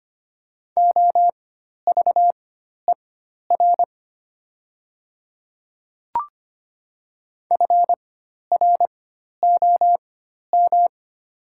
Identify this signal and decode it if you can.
Morse code, heard as a keyed tone, about 25 wpm, spaced as for 12 wpm, 700 Hz. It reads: OVER FROM
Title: 4x Character Spacing - Top 500 Words - 25wpm